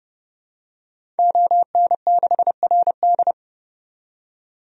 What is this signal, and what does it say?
Morse code, heard as a keyed tone, about 29 wpm, 700 Hz. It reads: ON6RD
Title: DXCC-100 Call Signs - 5 Characters - 30wpm